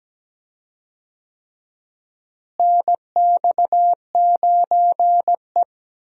Morse code, heard as a keyed tone, about 17 wpm, 700 Hz. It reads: NX9E
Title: USA Call Signs - 4 Characters - Repeated 3x - 17wpm